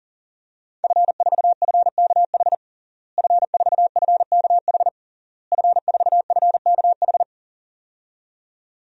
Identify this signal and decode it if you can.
Morse code, heard as a keyed tone, about 40 wpm, 700 Hz. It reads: F4FKH F4FKH F4FKH
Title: DXCC-100 Call Signs - 5 Characters - Repeated 3x - 40wpm